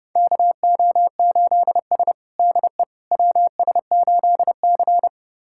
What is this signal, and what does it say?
Morse code, heard as a keyed tone, about 32 wpm, 700 Hz. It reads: KO8S DE WS8C
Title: USA Callsign DE Callsign - 4 Characters - 30wpm